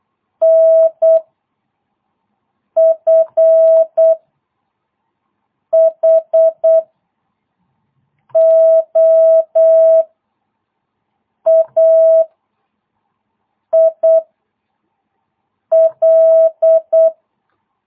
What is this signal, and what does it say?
Morse code, heard as a keyed tone, about 8 wpm, 650 Hz. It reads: N F H O A I L